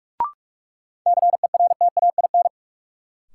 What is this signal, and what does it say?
Morse code, heard as a keyed tone, about 44 wpm, 700 Hz. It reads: CERTAIN
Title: Sentences from Top 500 Words - New Word Drill - 45wpm